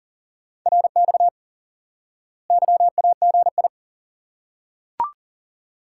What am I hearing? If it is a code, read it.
Morse code, heard as a keyed tone, about 40 wpm, 700 Hz. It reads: RX YAGI
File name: IQE Course 3 - Lesson 89 - YAGI - Two Elements - 40wpm